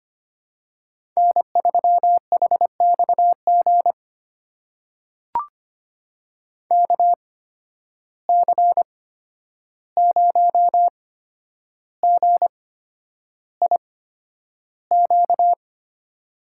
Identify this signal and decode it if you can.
Morse code, heard as a keyed tone, about 25 wpm, spaced as for 10 wpm, 700 Hz. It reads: N3HXG KC0GIQ
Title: USA Call Signs - 8x Character Spacing - 25wpm